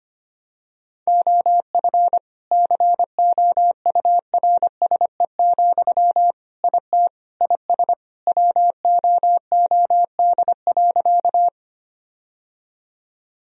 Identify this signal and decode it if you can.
Morse code, heard as a keyed tone, about 25 wpm, 700 Hz. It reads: OF COURSE, IT IS WOOD.
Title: Sentences from Top 1000 Words - Review Part 2 - 25wpm